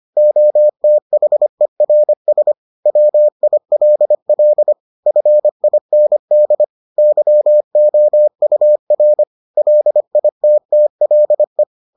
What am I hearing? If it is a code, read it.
Morse code, heard as a keyed tone, about 25 wpm, 600 Hz. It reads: OTHERS WILL FIND YOUR LITTLE